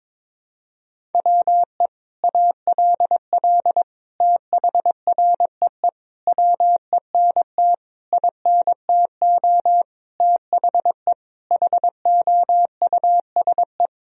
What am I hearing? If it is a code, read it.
Morse code, heard as a keyed tone, about 22 wpm, 700 Hz. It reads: WE ALL THREE WENT INTO THE HOUSE
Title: Sentences from Top 1000 Words - Review Part 1 - 22wpm